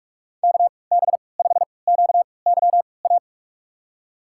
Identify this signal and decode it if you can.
Morse code, heard as a keyed tone, about 46 wpm, spaced as for 25 wpm, 700 Hz. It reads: KB5QYA
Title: USA Call Signs - 6 Characters - 3x Character Spacing - 45wpm